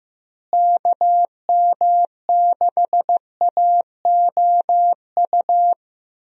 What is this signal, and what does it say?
Morse code, heard as a keyed tone, about 15 wpm, 700 Hz. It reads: KM6AOU
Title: USA Call Signs - 6 Characters - Repeated 3x - 15wpm